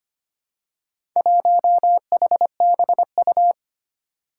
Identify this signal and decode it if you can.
Morse code, heard as a keyed tone, about 25 wpm, 700 Hz. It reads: 1HBU
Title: Four Challenging Letter-Number Combinations 25wpm